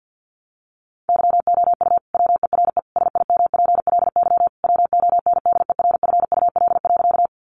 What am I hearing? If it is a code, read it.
Morse code, heard as a keyed tone, about 50 wpm, 700 Hz. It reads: YOU WERE SIMPLY WONDERFUL.
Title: Sentences from Top 1100 Words 50wpm